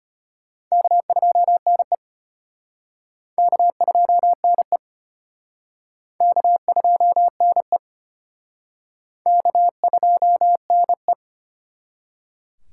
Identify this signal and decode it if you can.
Morse code, heard as a keyed tone, about 34 wpm, 700 Hz. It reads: K2NE K2NE K2NE K2NE